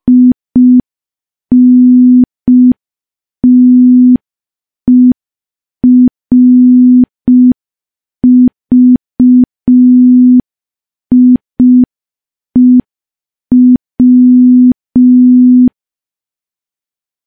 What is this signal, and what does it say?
Morse code, heard as a keyed tone, about 5 wpm, 250 Hz. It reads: INTERVIEW
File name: morse / morse-training-05WPM